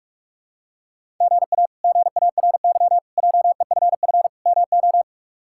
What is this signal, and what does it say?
Morse code, heard as a keyed tone, about 47 wpm, 700 Hz. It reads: GA GARY JEFF MO
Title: K1USN SST QSO Practice - 45wpm